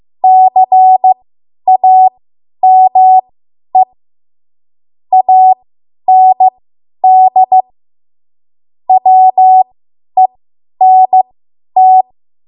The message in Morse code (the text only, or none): CAME AND WENT